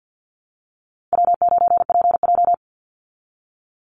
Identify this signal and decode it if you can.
Morse code, heard as a keyed tone, about 48 wpm, 700 Hz. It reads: W9PJ
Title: Easy Call Signs Repeated 3x - 50wpm